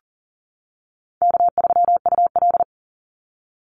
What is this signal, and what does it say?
Morse code, heard as a keyed tone, about 40 wpm, 700 Hz. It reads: K3UL
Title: Easy Call Signs - 40wpm